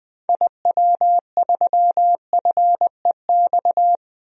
Morse code, heard as a keyed tone, about 21 wpm, 700 Hz. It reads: IW3FEX
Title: DXCC-100 Call Signs - 6 Characters - 20wpm